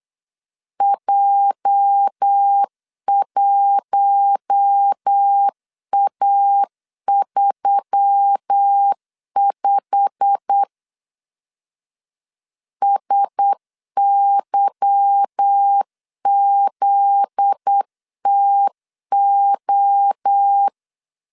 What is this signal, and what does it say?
Morse code, heard as a keyed tone, about 8 wpm, 800 Hz. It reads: J1A35 SYZTO
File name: LEZIONI / 8 YJ45 / 8 YJ45 8wpm 3214